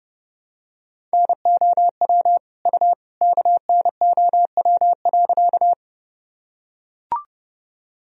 Morse code, heard as a keyed tone, about 30 wpm, 700 Hz. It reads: NOW U KNOW.
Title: Sentences from Top 100 Words and Abbreviations 30wpm